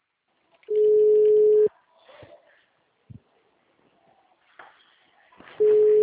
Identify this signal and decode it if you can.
Ringback tone (Europe, 425 Hz).